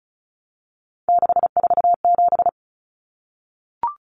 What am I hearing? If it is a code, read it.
Morse code, heard as a keyed tone, about 34 wpm, 700 Hz. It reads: 647
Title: Numbers - 3-Digit 35wpm